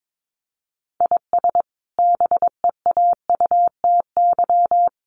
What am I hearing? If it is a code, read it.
Morse code, heard as a keyed tone, about 22 wpm, 700 Hz. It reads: IS BEAUTY
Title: Sentences from Top 1300 Words 22wpm